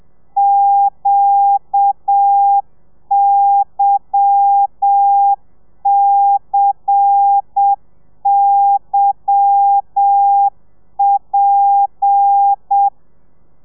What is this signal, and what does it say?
Morse code, heard as a keyed tone, about 7 wpm, 800 Hz. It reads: QYCYP
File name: telegrafia / i2tfj / Lez10 30